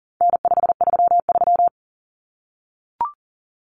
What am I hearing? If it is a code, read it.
Morse code, heard as a keyed tone, about 40 wpm, 700 Hz. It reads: N533
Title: Common Cut Numbers - 4-Digit 40wpm